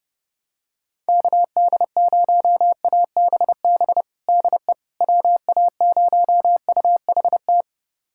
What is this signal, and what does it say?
Morse code, heard as a keyed tone, about 30 wpm, 700 Hz. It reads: KD0ABB DE WA0UHT
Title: USA Callsign DE Callsign - 6 Characters - 30wpm